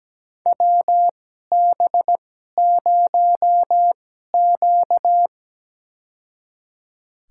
Morse code, heard as a keyed tone, about 17 wpm, spaced as for 12 wpm, 700 Hz. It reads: WB0Q
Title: USA Call Signs - 4 Characters - 2x Character Spacing - 17wpm